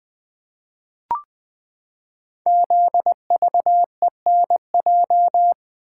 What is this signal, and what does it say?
Morse code, heard as a keyed tone, about 20 wpm, 700 Hz. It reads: ZVENJ
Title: Code Groups - 5 Letters - 20wpm